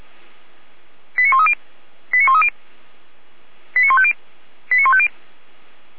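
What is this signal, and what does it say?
Five-tone selective call (ZVEI), heard as quick runs of five tones, tones of about 70 ms. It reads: 89130 89130 89150 89150
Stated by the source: ZVEI five-tone call received by SDR.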